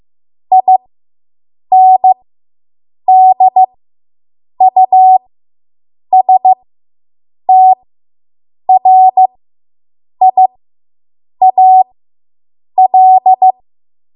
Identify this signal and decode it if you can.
Morse code, heard as a keyed tone, about 15 wpm, spaced as for 7 wpm, 750 Hz. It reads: INDUSTRIAL